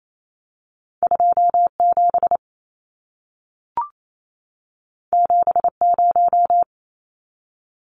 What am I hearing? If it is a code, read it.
Morse code, heard as a keyed tone, about 28 wpm, 700 Hz. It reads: 27 70
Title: Numbers - 2-Digit 28wpm